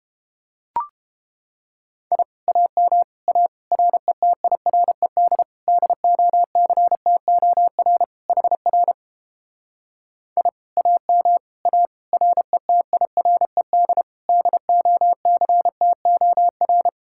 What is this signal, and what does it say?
Morse code, heard as a keyed tone, about 29 wpm, 700 Hz. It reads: I AM A RETIRED DOCTOR HR I AM A RETIRED DOCTOR